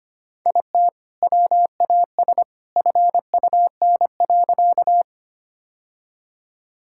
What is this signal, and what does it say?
Morse code, heard as a keyed tone, about 25 wpm, 700 Hz. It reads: IT WAS FUN.
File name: practice / 25wpm / Sentences with 2-Word Phrases - 25wpm